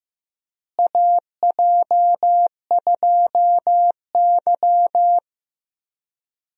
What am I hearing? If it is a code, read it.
Morse code, heard as a keyed tone, about 15 wpm, 700 Hz. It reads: AJ2Y
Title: USA Call Signs - 4 Characters - Repeated 3x - 15wpm